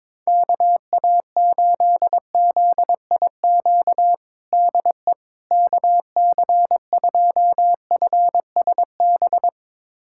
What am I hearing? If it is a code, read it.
Morse code, heard as a keyed tone, about 22 wpm, 700 Hz. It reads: KA8ZIQ DE KC2FSB
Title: USA Callsign DE Callsign 22wpm